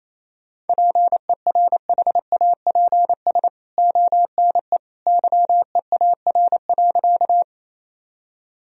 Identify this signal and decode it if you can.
Morse code, heard as a keyed tone, about 28 wpm, 700 Hz. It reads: PERHAPS ONE YEAR.
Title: Sentences from Top 1000 Words - Review Part 1 - 28wpm